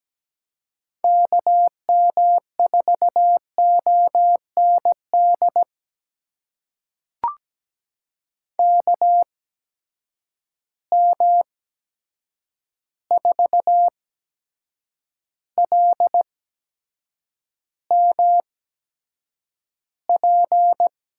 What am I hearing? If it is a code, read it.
Morse code, heard as a keyed tone, about 17 wpm, spaced as for 7 wpm, 700 Hz. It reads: KM4OND KM4LMP
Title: USA Call Signs - 6 Characters - 8x Character Spacing - 17wpm